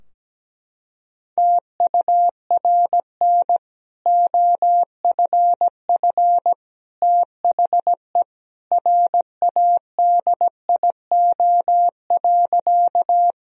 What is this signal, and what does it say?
Morse code, heard as a keyed tone, about 17 wpm, 700 Hz. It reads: TURN OFF THE RADIO.
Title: Sentences from Top 1000 Words - Review Part 3 - 17wpm